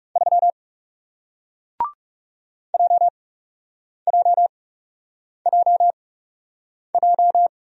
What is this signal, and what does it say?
Morse code, heard as a keyed tone, about 40 wpm, spaced as for 8 wpm, 700 Hz. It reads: 3 JJJJ